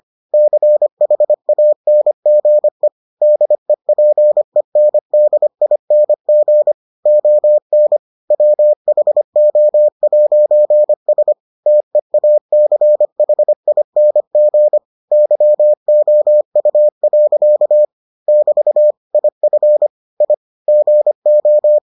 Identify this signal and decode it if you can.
Morse code, heard as a keyed tone, about 25 wpm, 600 Hz. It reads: CHANGE DEPENDING ON WHO'S TEACHING YOU. <BT> IF I GO